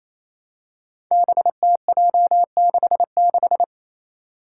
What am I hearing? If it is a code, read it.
Morse code, heard as a keyed tone, about 28 wpm, 700 Hz. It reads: BTJ66
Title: Five Challenging Letter-Number Combinations 28wpm